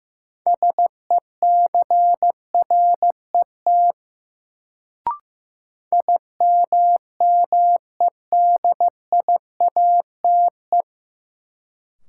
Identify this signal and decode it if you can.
Morse code, heard as a keyed tone, about 15 wpm, 700 Hz. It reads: SECRET IMMEDIATE
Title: Sentences from Top 1200 Words - New Word Drill Alt - 15wpm